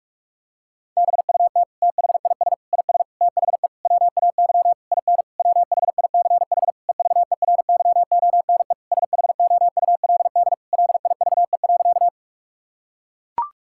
Morse code, heard as a keyed tone, about 43 wpm, 700 Hz. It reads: BUT THIS IS THE WAY IN WHICH EVERYONE SHOULD LIVE.
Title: Sentences from Top 1000 Words - Review Part 5 - 45wpm